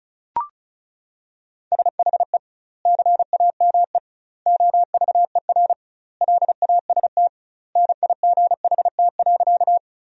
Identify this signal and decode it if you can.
Morse code, heard as a keyed tone, about 36 wpm, 700 Hz. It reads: SHE CAME OVER LAST NIGHT.
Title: 2x Word-Spacing - Sentences from Top 300 Words - 35wpm